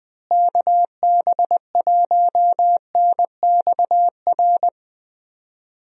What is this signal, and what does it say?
Morse code, heard as a keyed tone, about 20 wpm, 700 Hz. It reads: KB1NXR
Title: USA Call Signs - 6 Characters - Repeated 3x - 20wpm